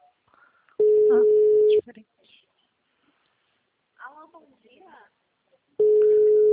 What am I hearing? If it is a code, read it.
Ringback tone (Europe, 425 Hz), with speech.
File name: out- -2016- - - .62264